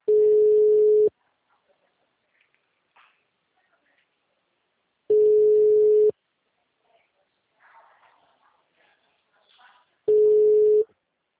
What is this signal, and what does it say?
Ringback tone (Europe, 425 Hz).